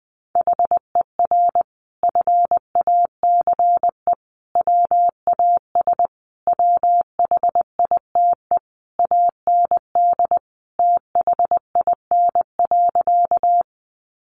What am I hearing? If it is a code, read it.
Morse code, heard as a keyed tone, about 20 wpm, 700 Hz. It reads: HER FACE WAS WHITE AND THIN.
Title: Sentences from Top 1200 Words 20wpm